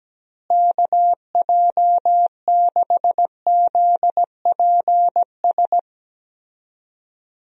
Morse code, heard as a keyed tone, about 17 wpm, 700 Hz. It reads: KJ6ZPS
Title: USA Call Signs - 6 Characters - Repeated 2x - 17wpm